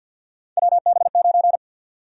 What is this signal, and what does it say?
Morse code, heard as a keyed tone, about 52 wpm, 700 Hz. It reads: WB9
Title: USA Call Sign Prefixes - 50wpm